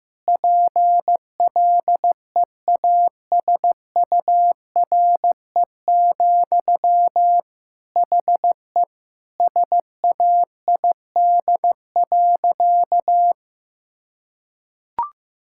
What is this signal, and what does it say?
Morse code, heard as a keyed tone, about 15 wpm, 700 Hz. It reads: PLEASURE, HE SAID.
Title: Sentences from Top 1600 Words 15wpm